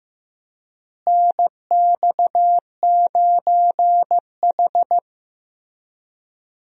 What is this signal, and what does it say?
Morse code, heard as a keyed tone, about 15 wpm, 700 Hz. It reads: NX9H
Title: USA Call Signs - 4 Characters - Repeated 3x - 15wpm